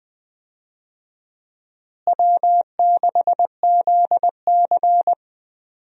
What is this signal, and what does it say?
Morse code, heard as a keyed tone, about 20 wpm, 700 Hz. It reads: W6ZC
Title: USA Call Signs - 4 Characters - Repeated 2x - 20wpm